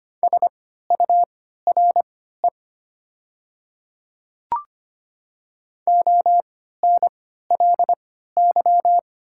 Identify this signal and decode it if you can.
Morse code, heard as a keyed tone, about 25 wpm, spaced as for 14 wpm, 700 Hz. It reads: SURE ONLY
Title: 3x Character Spacing - Top 300 Words - 25wpm